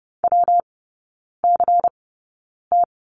Morse code, heard as a keyed tone, about 30 wpm, spaced as for 9 wpm, 700 Hz. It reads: WCT
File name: Lesson 19e - Letter w - Three Characters 30wpm